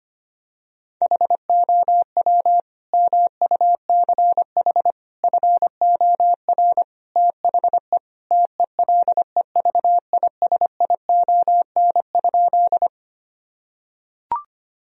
Morse code, heard as a keyed tone, about 25 wpm, 700 Hz. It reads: HOW MUCH FOR THE TELEVISION?
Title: Sentences from Top 1000 Words - Review Part 4 - 25wpm